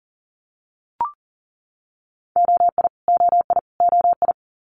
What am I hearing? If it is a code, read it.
Morse code, heard as a keyed tone, about 40 wpm, 690 Hz. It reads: OI OI OI
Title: Multi-letter Phonemes Repeated 3x 40wpm